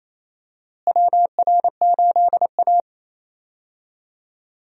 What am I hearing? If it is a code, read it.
Morse code, heard as a keyed tone, about 27 wpm, 700 Hz. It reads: WR8A
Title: USA Call Signs - 4 Characters - Repeated 2x - 28wpm